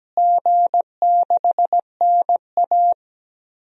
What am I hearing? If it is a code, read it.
Morse code, heard as a keyed tone, about 17 wpm, 700 Hz. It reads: G6NA
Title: DXCC-100 Call Signs - 4 Characters - Repeated 2x - 17wpm